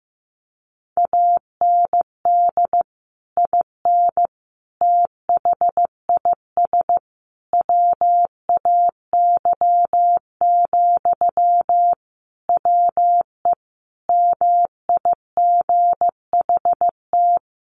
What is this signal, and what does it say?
Morse code, heard as a keyed tone, about 15 wpm, 700 Hz. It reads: AND IN THIS WAY, WE MIGHT
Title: Sentences from Top 1400 Words 15wpm